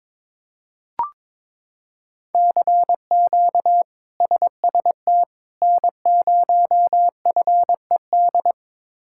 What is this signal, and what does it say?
Morse code, heard as a keyed tone, about 22 wpm, 700 Hz. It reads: CQ SST N0FED
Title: K1USN SST QSO Practice - 22wpm